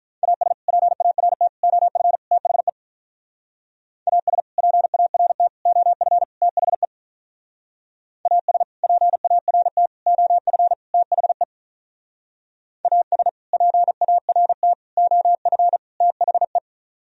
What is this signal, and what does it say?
Morse code, heard as a keyed tone, about 44 wpm, 700 Hz. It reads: AS PART OF THE AS PART OF THE AS PART OF THE AS PART OF THE
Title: Speed-Racing - Sets of 4 Words 35wpm